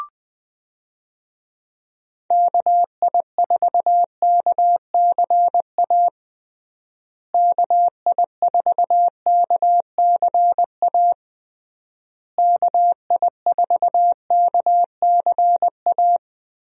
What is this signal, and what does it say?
Morse code, heard as a keyed tone, about 19 wpm, 700 Hz. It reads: KI4KCA KI4KCA KI4KCA
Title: USA Call Signs - 6 Characters - Repeated 3x - 20wpm